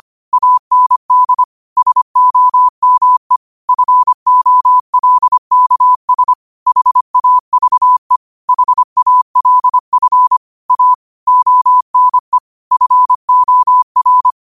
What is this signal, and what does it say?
Morse code, heard as a keyed tone, about 25 wpm, 1000 Hz. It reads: AND SOME FOLKS HAVE HALF A ONE FOR
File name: qotdfiles / 25wpm